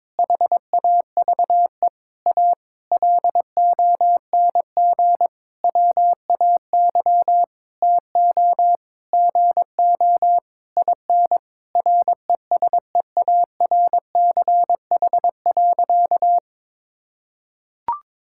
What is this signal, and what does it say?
Morse code, heard as a keyed tone, about 22 wpm, 700 Hz. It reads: HAVE A LONG WAY TO GO IN RESEARCH.
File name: Sentences with 9-Word Phrases - 22wpm